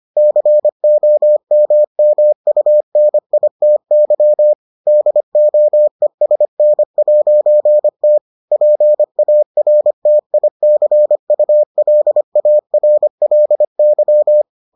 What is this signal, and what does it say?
Morse code, heard as a keyed tone, about 25 wpm, 600 Hz. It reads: COMMUNITY DOESN'T PARTICULARLY